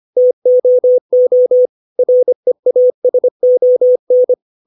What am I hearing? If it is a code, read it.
Morse code, heard as a keyed tone, about 25 wpm, 500 Hz. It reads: TOO REASON
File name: 632 words / 632 Words 25WPM